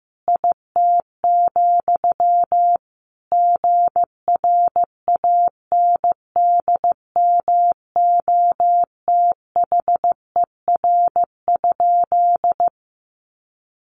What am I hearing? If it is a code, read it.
Morse code, heard as a keyed tone, about 15 wpm, 700 Hz. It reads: IT, GRANDMOTHER?
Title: Sentences from Top 1200 Words 15wpm